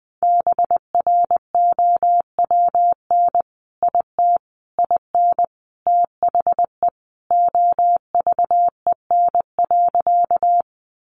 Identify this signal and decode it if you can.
Morse code, heard as a keyed tone, about 19 wpm, 700 Hz. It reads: BROWN IT IN THE OVEN.
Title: Sentences from Top 1200 Words 20wpm